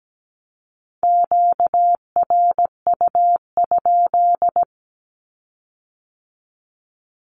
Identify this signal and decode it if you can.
Morse code, heard as a keyed tone, about 17 wpm, 700 Hz. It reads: QRU?
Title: Q-Codes with Spoken Meaning - 17wpm